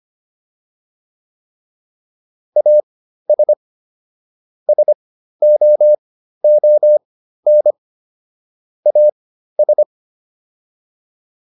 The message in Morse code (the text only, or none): AS SOON AS